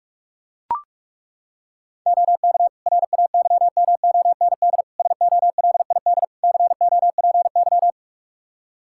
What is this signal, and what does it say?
Morse code, heard as a keyed tone, about 42 wpm, 700 Hz. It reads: OK RAYMOND SOLID COPY
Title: Practice QSO - Exchange 3 - More Info - 45wpm